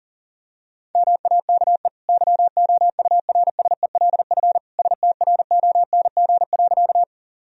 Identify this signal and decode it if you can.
Morse code, heard as a keyed tone, about 40 wpm, 700 Hz. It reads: MAKE YOURSELF STRONG.
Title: Sentences from Top 1000 Words - Review Part 3 - 40wpm